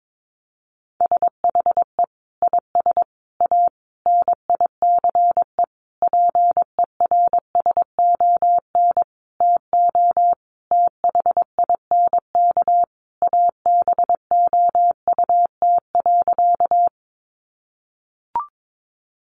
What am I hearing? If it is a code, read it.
Morse code, heard as a keyed tone, about 22 wpm, 700 Hz. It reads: SHE IS A NICE PERSON TO THINK ABOUT.
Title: Sentences from Top 1100 Words 22wpm